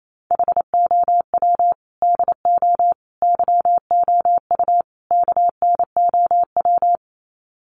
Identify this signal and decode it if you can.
Morse code, heard as a keyed tone, about 28 wpm, 700 Hz. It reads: HOW DO YOU KNOW
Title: Sets of 4 Words 28wpm